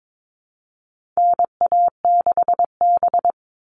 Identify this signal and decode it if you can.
Morse code, heard as a keyed tone, about 22 wpm, 700 Hz. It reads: NA6B
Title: All Cut Numbers - 4-Digit 22wpm